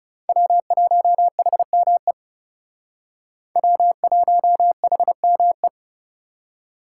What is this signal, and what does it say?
Morse code, heard as a keyed tone, about 37 wpm, 700 Hz. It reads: W1HME W1HME
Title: Speed Racing - USA 1x3 Call Signs 30wpm